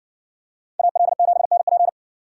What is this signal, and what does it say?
Morse code, heard as a keyed tone, about 57 wpm, 700 Hz. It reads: AL7NP